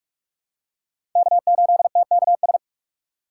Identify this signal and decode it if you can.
Morse code, heard as a keyed tone, about 44 wpm, 700 Hz. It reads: K8TKS
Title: USA Call Signs - 5 Characters - 4x Character Spacing - 45wpm